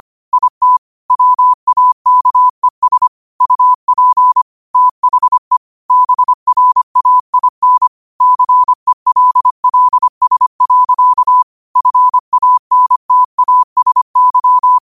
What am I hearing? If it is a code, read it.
Morse code, heard as a keyed tone, about 25 wpm, 1000 Hz. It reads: IT WAKES UP THE BRAIN CELLS. FANTASY
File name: qotdfiles / 25wpm